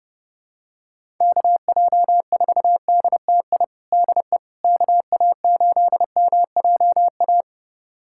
Morse code, heard as a keyed tone, about 30 wpm, 700 Hz. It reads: KJ4DTI DE KA8MJA
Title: USA Callsign DE Callsign 30wpm